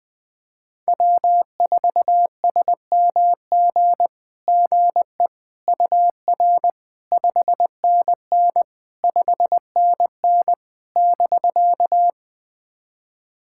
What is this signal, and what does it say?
Morse code, heard as a keyed tone, about 20 wpm, 690 Hz. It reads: W4SMG GE UR 5NN 5NN <BK>
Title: SOTA QSO Practice 20wpm